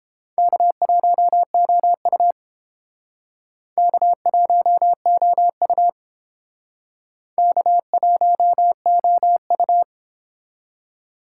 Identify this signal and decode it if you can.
Morse code, heard as a keyed tone, about 29 wpm, 700 Hz. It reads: K1OU K1OU K1OU